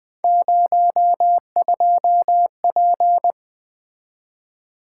Morse code, heard as a keyed tone, about 20 wpm, 700 Hz. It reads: 02P
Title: Three Letter-Number Combinations - 20wpm